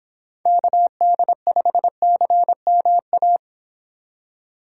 Morse code, heard as a keyed tone, about 26 wpm, 700 Hz. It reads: KD5CMA
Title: Speed Racing - USA 2x3 Call Signs 22wpm